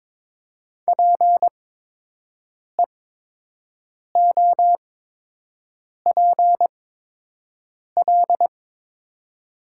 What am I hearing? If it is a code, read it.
Morse code, heard as a keyed tone, about 22 wpm, spaced as for 6 wpm, 700 Hz. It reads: PEOPL
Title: 8x Character Spacing - Top 100 Words - 22wpm